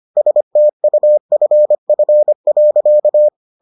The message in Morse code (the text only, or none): STUFF.